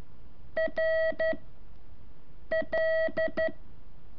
Morse code, heard as a keyed tone, about 13 wpm, 640 Hz. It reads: R L
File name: curso / ejercicio 3